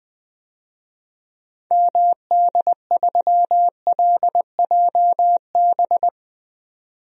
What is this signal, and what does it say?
Morse code, heard as a keyed tone, about 20 wpm, 700 Hz. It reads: MD3LJB
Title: DXCC-100 Call Signs - 6 Characters - 20wpm